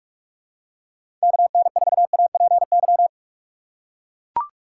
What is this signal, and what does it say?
Morse code, heard as a keyed tone, about 42 wpm, 700 Hz. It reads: KN4APY
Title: USA Call Signs - 2x Character Spacing - 45wpm